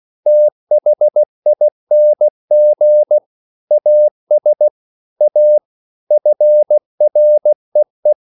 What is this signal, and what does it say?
Morse code, heard as a keyed tone, about 16 wpm, 600 Hz. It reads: THING AS A FREE
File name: CW/Practice